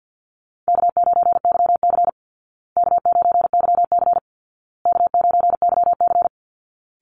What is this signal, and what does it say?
Morse code, heard as a keyed tone, about 50 wpm, 700 Hz. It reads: K9YC K9YC K9YC
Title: Easy Call Signs Repeated 3x - 50wpm